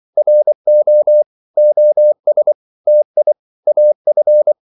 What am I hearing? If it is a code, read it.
Morse code, heard as a keyed tone, about 24 wpm, 600 Hz. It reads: RO OS TI AF
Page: Korte danske ord 24wpm